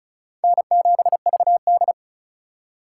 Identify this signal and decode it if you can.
Morse code, heard as a keyed tone, about 36 wpm, 700 Hz. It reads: N7VD
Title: USA Call Signs - 2x Character Spacing - 35wpm